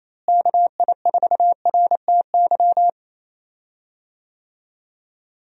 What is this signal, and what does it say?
Morse code, heard as a keyed tone, about 28 wpm, 700 Hz. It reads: KI4RTY
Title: USA Call Signs - 6 Characters - Repeated 2x - 28wpm